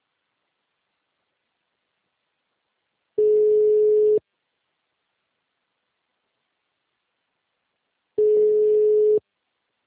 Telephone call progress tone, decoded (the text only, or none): ringback tone